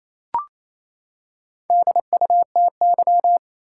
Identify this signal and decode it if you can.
Morse code, heard as a keyed tone, about 28 wpm, 700 Hz. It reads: DUTY
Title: Four Letter Words - 28wpm